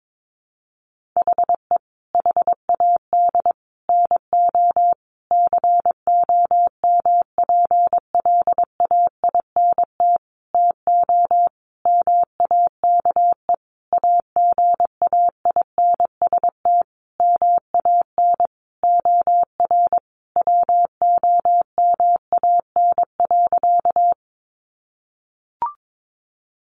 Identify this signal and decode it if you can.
Morse code, heard as a keyed tone, about 22 wpm, 700 Hz. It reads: HE HAD NO COMPLAINT TO MAKE AGAINST MAN OR WOMAN.
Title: Sentences from Top 1400 Words 22wpm